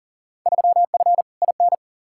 Morse code, heard as a keyed tone, about 40 wpm, 700 Hz. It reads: 3F IN
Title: Field Day Exchanges - 40wpm